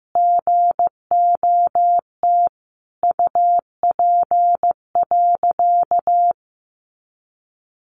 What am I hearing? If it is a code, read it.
Morse code, heard as a keyed tone, about 15 wpm, 700 Hz. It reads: GOT UP.
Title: A Sound Of Thunder 15wpm